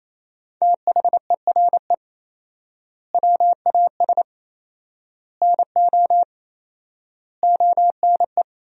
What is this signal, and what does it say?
Morse code, heard as a keyed tone, about 28 wpm, 700 Hz. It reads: THERE WAS NO ONE
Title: 4x Word-Spacing - Sentences from Top 200 Words - 28wpm